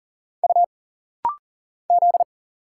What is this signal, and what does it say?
Morse code, heard as a keyed tone, about 40 wpm, 700 Hz. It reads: U Z
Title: Single Letters - Rapid-Fire 40wpm